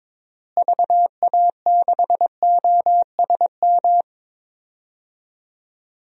Morse code, heard as a keyed tone, about 22 wpm, 700 Hz. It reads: VA6OSM
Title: DXCC-100 Call Signs - 6 Characters - Repeated 2x - 22wpm